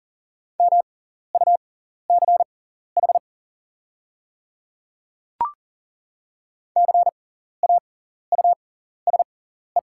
Morse code, heard as a keyed tone, about 38 wpm, spaced as for 14 wpm, 690 Hz. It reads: MUCH CAUSE
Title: 6x Character Spacing - Top 200 Words - 40wpm